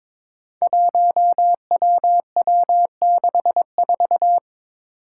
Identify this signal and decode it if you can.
Morse code, heard as a keyed tone, about 22 wpm, 700 Hz. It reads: 1WW64
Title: Five Challenging Letter-Number Combinations 22wpm